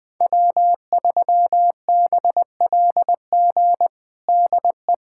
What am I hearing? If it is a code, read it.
Morse code, heard as a keyed tone, about 20 wpm, 700 Hz. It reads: W3BLG DE
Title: USA Callsign DE Callsign - 5 Characters - 20wpm